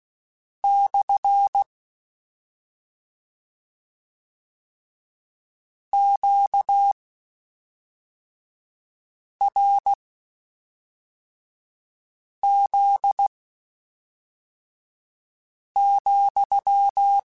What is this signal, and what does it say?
Morse code, heard as a keyed tone, about 16 wpm, spaced as for 3 wpm, 800 Hz. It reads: / QRZ,